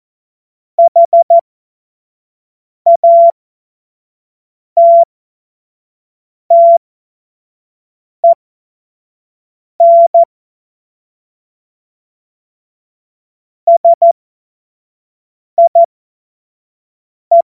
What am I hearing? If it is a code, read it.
Morse code, heard as a keyed tone, about 14 wpm, spaced as for 5 wpm, 680 Hz. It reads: HATTEN SIE
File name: Morsekurs / Witze / Witz 13 Februar 2024 25 F70